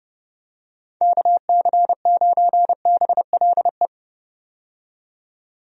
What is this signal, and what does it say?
Morse code, heard as a keyed tone, about 30 wpm, 700 Hz. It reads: KC9BLE
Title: USA Call Signs - 6 Characters - 30wpm